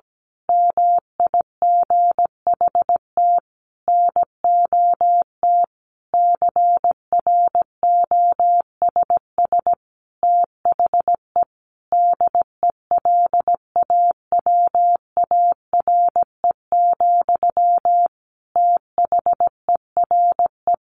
Morse code, heard as a keyed tone, about 17 wpm, 700 Hz. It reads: MIGHT NOT CROSS THE DELAWARE, THERE
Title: A Sound Of Thunder 17wpm